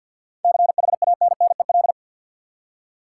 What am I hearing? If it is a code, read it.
Morse code, heard as a keyed tone, about 53 wpm, 700 Hz. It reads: CHANNEL
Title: Seven Letter Words - 50wpm